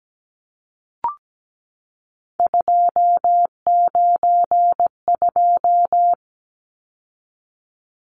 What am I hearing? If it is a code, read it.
Morse code, heard as a keyed tone, about 17 wpm, 700 Hz. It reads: 292
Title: Numbers - 3-Digit 17wpm